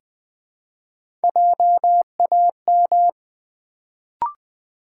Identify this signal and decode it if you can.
Morse code, heard as a keyed tone, about 20 wpm, 700 Hz. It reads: JAM
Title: Build A Word - 1 Letter At A Time - 3-Letter Words - 20wpm